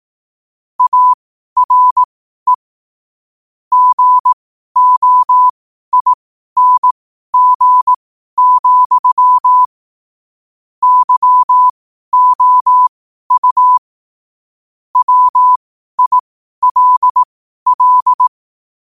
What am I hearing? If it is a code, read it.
Morse code, heard as a keyed tone, about 18 wpm, spaced as for 12 wpm, 1000 Hz. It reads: ARE GOING, YOU WILL
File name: qotdfiles / 12wpm